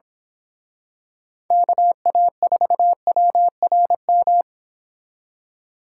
Morse code, heard as a keyed tone, about 26 wpm, 700 Hz. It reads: KA4WRM
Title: Speed Racing - USA 2x3 Call Signs 22wpm